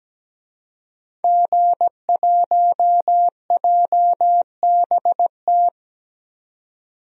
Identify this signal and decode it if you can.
Morse code, heard as a keyed tone, about 17 wpm, 700 Hz. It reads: G1JBT
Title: DXCC-100 Call Signs - 5 Characters - 17wpm